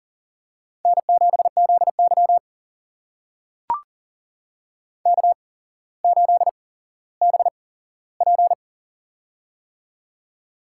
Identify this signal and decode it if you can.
Morse code, heard as a keyed tone, about 40 wpm, spaced as for 17 wpm, 700 Hz. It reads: N7ZY K8BP
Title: USA Call Signs - 8x Character Spacing - 40wpm